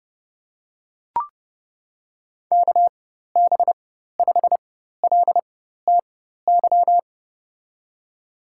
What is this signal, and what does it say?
Morse code, heard as a keyed tone, about 30 wpm, spaced as for 14 wpm, 700 Hz. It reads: KB5LTY
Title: USA Call Signs - 4x Character Spacing - 30wpm